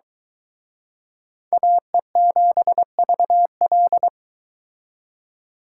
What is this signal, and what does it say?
Morse code, heard as a keyed tone, about 22 wpm, 700 Hz. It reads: AE7VL